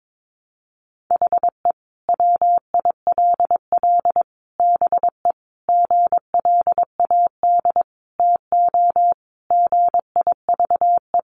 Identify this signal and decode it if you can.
Morse code, heard as a keyed tone, about 22 wpm, 700 Hz. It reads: HE WILL BE GLAD TO GIVE
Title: Sentences from Top 800 Words 22wpm